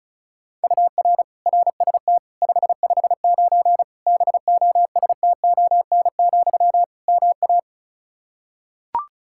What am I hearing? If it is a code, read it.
Morse code, heard as a keyed tone, about 36 wpm, 700 Hz. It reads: UR RST 559 BOSTON, MA
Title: Practice QSO - Exchange 1 - Beginning - 35wpm